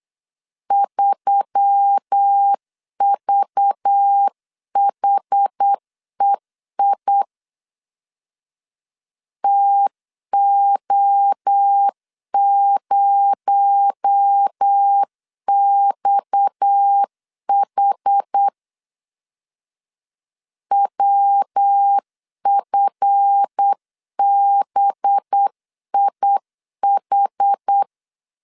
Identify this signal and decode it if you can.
Morse code, heard as a keyed tone, about 8 wpm, 800 Hz. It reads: 3VHEI TO0XH WFBIH
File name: LEZIONI / 7 GF9Z / 7 GF 8wpm 3108